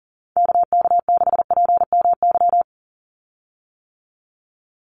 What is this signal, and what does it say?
Morse code, heard as a keyed tone, about 40 wpm, 700 Hz. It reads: KK6PMY
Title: USA Call Signs 40wpm